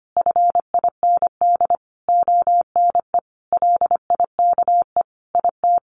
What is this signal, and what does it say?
Morse code, heard as a keyed tone, about 25 wpm, 700 Hz. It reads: FIND ONE LIKE IT
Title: Sentences from Top 1400 Words 25wpm